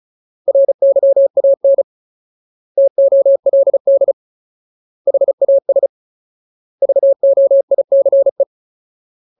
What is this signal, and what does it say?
Morse code, heard as a keyed tone, about 35 wpm, 550 Hz. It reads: RYAN TOLD HAS VOICE
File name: Ultimate Word List / 35WPM Ultimate